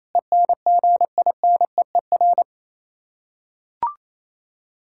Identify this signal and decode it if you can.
Morse code, heard as a keyed tone, about 28 wpm, 700 Hz. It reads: ENGINEER
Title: Practice QSO - Exchange 3 - More Info - 28wpm